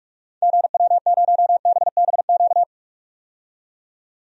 Morse code, heard as a keyed tone, about 43 wpm, 700 Hz. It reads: GW0BBQ
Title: DXCC-100 Call Signs - 6 Characters - 45wpm